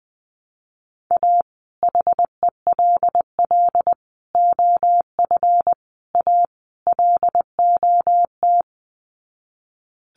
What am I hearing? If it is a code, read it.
Morse code, heard as a keyed tone, about 20 wpm, 700 Hz. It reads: A HELL OF A LOT
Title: Sets of 5 Words Encore 20wpm